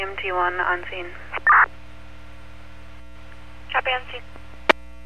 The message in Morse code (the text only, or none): none